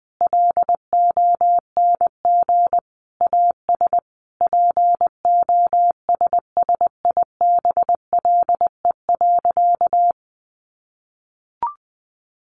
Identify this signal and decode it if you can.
Morse code, heard as a keyed tone, about 20 wpm, 700 Hz. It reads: LONG AS POSSIBLE.
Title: Sentences from Top 900 Words 20wpm